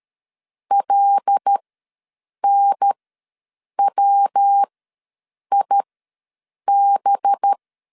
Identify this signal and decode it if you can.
Morse code, heard as a keyed tone, about 13 wpm, spaced as for 7 wpm, 800 Hz. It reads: LNWIB
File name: LEZIONI / 7 GF9Z / 7 G ws 12wpm 3140